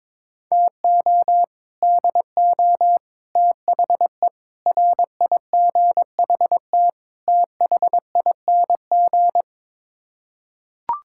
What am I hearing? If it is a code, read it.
Morse code, heard as a keyed tone, about 22 wpm, 700 Hz. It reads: TO DO THE RIGHT THING
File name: practice / 22wpm / Build A Phrase - 5-Word Phrases - 22wpm